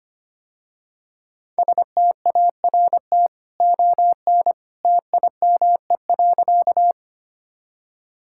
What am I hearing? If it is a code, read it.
Morse code, heard as a keyed tone, about 25 wpm, 700 Hz. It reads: START ON TIME.
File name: Easy Sentences with Hints - 25wpm